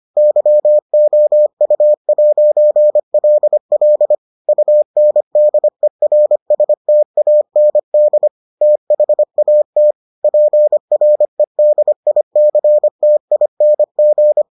Morse code, heard as a keyed tone, about 25 wpm, 600 Hz. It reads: YOU'LL UNDERSTAND THAT PREDICTING